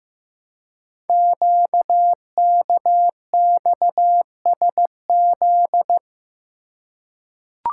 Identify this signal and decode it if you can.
Morse code, heard as a keyed tone, about 15 wpm, 700 Hz. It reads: QKXSZ
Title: Code Groups - 5 Letters - 15wpm